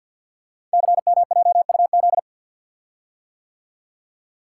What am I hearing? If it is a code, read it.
Morse code, heard as a keyed tone, about 50 wpm, 690 Hz. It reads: CMJUZ
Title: Code Groups - 5 Letters - 50wpm